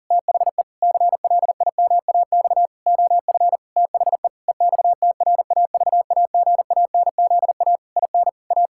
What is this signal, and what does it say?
Morse code, heard as a keyed tone, about 40 wpm, 700 Hz. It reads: THE CLIMAX OF THE EXTRAVAGANZA IN A
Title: Pangram Challenge v2 - 40wpm